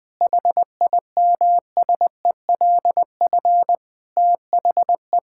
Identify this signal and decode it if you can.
Morse code, heard as a keyed tone, about 20 wpm, 700 Hz. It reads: HIMSELF THE
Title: Sentences from Top 1600 Words 20wpm